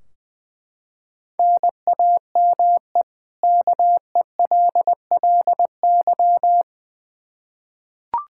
Morse code, heard as a keyed tone, about 20 wpm, 700 Hz. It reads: NAME KELLY
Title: Practice QSO - Exchange 1 - Beginning - 20wpm